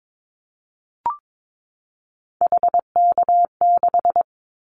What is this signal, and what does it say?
Morse code, heard as a keyed tone, about 22 wpm, 700 Hz. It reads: HK6
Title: Callsign Prefixes for Top 100 DXCC - 22wpm